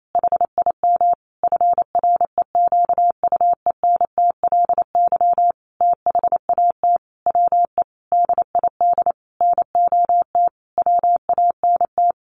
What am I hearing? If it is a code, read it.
Morse code, heard as a keyed tone, about 28 wpm, 700 Hz. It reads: HIM FREQUENTLY THAT WE DID NOT WANT
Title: Sentences from Top 1300 Words 28wpm